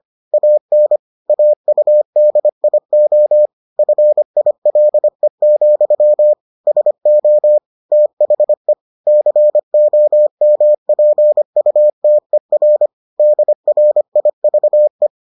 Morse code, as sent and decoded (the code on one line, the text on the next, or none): .- -. / .- ..- -.. .. --- / ..-. .. .-.. --..-- / ... --- / - .... / -.-. --- -- .--. ..- - . .-. / -.. .-. .. ...- .
AN AUDIO FILE, SO THE COMPUTER DRIVE